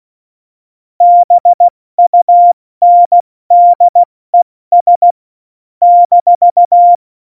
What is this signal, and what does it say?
Morse code, heard as a keyed tone, about 16 wpm, 700 Hz. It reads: BUNDES -
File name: Morsekurs / Witze / Witz 14 Juli 2020, 70 F80